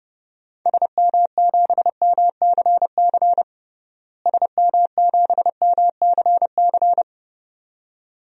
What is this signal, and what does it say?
Morse code, heard as a keyed tone, about 32 wpm, 700 Hz. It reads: SM7MCC SM7MCC
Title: DXCC-100 Call Signs - 6 Characters - Repeated 2x - 30wpm